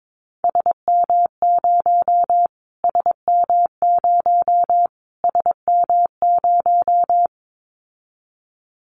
Morse code, heard as a keyed tone, about 22 wpm, 700 Hz. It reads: SM0 SM0 SM0
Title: Callsign Prefixes for Top 25 DXCC - Repeated 3x - 22wpm